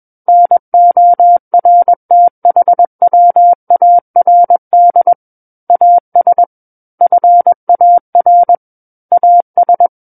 Morse code, heard as a keyed tone, about 21 wpm, 700 Hz. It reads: NORTHWARD AS FAR AS
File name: Book One / 12-Chapter 12 - 21 WPM